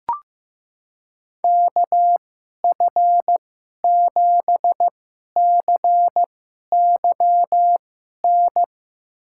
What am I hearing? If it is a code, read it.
Morse code, heard as a keyed tone, about 15 wpm, spaced as for 11 wpm, 700 Hz. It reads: KF7CYN
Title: USA Call Signs - 6 Characters - 2x Character Spacing - 15wpm